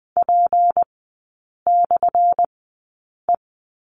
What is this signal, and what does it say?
Morse code, heard as a keyed tone, about 20 wpm, spaced as for 8 wpm, 700 Hz. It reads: P/E
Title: 178 Lesson 29e - Slash - Three Characters 20wpm